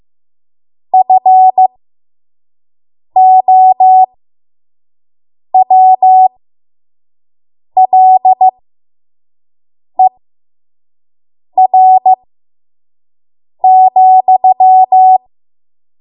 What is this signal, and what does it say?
Morse code, heard as a keyed tone, about 15 wpm, spaced as for 5 wpm, 750 Hz. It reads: FOWLER,
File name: Morse / 05WPM